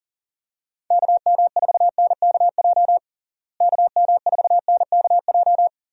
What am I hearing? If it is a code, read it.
Morse code, heard as a keyed tone, about 40 wpm, 700 Hz. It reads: KM4NKJ KM4NKJ
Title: USA Call Signs - 6 Characters - Repeated 2x - 40wpm